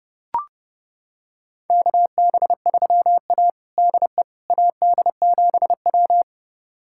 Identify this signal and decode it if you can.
Morse code, heard as a keyed tone, about 32 wpm, 700 Hz. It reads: KB3A DE AD7W
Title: USA Callsign DE Callsign - 4 Characters - 30wpm